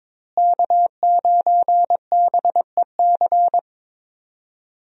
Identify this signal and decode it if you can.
Morse code, heard as a keyed tone, about 22 wpm, 700 Hz. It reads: K9BEC
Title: USA Call Signs - 5 Characters - Repeated 3x - 22wpm